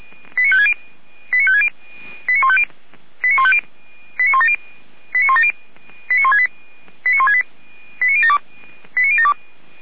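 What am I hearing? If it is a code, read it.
Five-tone selective call (ZVEI), heard as quick runs of five tones, tones of about 70 ms. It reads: 89560 89560 89150 89150 89170 89170 89168 89072 89072
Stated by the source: ZVEI five-tone selective call received by SDR, from feuerwehr-orlamuende.org (Feuerwehr Orlamünde) fire brigade.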